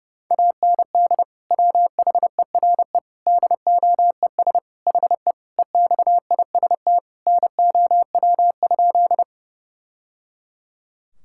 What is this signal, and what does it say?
Morse code, heard as a keyed tone, about 32 wpm, 700 Hz. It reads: AND WHERE DOES HE EXIST NOW?
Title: Sentences from Top 1000 Words - Review Part 5 - 30wpm